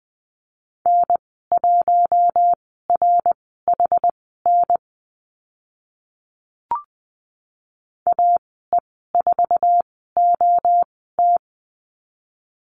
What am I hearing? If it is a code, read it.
Morse code, heard as a keyed tone, about 20 wpm, spaced as for 14 wpm, 700 Hz. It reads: N1RHN AE4OT
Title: 069 Lesson 12g - Letter d - Call Signs 20wpm